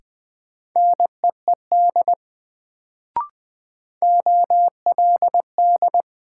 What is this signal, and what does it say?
Morse code, heard as a keyed tone, about 20 wpm, 700 Hz. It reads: NEED OLD
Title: Sets of 2 Words - New Word Drill - 20wpm